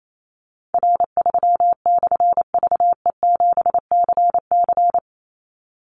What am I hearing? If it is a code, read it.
Morse code, heard as a keyed tone, about 28 wpm, 700 Hz. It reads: R3/VE7CC
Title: Call Signs - 28wpm